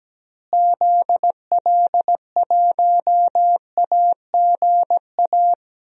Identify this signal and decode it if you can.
Morse code, heard as a keyed tone, about 17 wpm, 700 Hz. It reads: ZL1AGA
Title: DXCC-100 Call Signs - 6 Characters - Repeated 3x - 17wpm